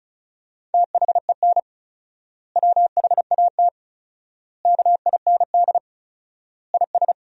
Morse code, heard as a keyed tone, about 36 wpm, 700 Hz. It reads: THEN WHAT KIND IS
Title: 4x Word-Spacing - Sentences from Top 200 Words - 35wpm